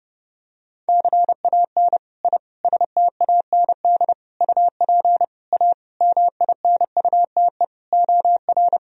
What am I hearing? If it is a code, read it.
Morse code, heard as a keyed tone, about 30 wpm, 700 Hz. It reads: CAN I STAND UP A MINUTE OR
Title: Sentences from Top 1000 Words - Review Part 1 - 30wpm